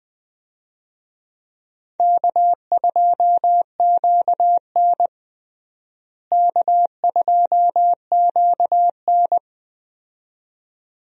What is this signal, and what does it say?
Morse code, heard as a keyed tone, about 20 wpm, 700 Hz. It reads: K2QN K2QN
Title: USA Call Signs - 4 Characters - Repeated 2x - 20wpm